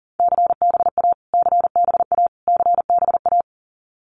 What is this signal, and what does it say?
Morse code, heard as a keyed tone, about 40 wpm, 700 Hz. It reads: CBA CBA CBA
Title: Abbreviations Repeated 3x - 40wpm